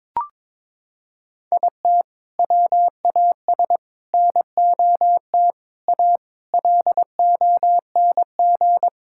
Morse code, heard as a keyed tone, about 22 wpm, 700 Hz. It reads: IT WAS NOT A LONG
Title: Sentences from Top 1000 Words - Review Part 6 - 22wpm